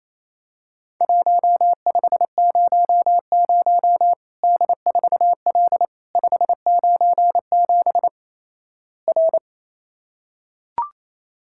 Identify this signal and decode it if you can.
Morse code, heard as a keyed tone, about 28 wpm, 700 Hz. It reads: 1500 D4L 597 R
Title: Worked All Europe DX Contest - QTC Exchange - 28wpm